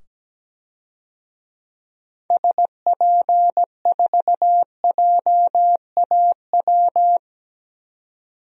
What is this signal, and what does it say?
Morse code, heard as a keyed tone, about 17 wpm, 700 Hz. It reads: SP4JAW
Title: DXCC-100 Call Signs - 6 Characters - 17wpm